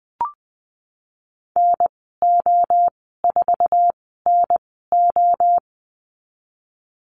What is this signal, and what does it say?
Morse code, heard as a keyed tone, about 20 wpm, spaced as for 14 wpm, 700 Hz. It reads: NO4NO
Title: 055 Lesson 10g - Letter r - Call Signs 20wpm